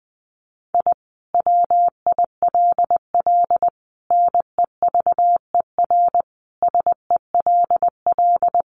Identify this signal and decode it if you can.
Morse code, heard as a keyed tone, about 20 wpm, 700 Hz. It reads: I WILL NEVER SELL
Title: Sentences from Top 1100 Words 20wpm